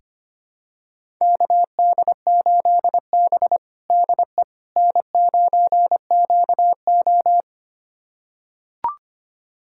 Morse code, heard as a keyed tone, about 25 wpm, 700 Hz. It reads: KD8B DE N9QO
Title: USA Callsign DE Callsign - 4 Characters - 25wpm